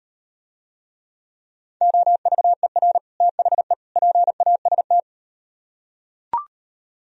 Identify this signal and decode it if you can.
Morse code, heard as a keyed tone, about 38 wpm, 700 Hz. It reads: OVER THE PAST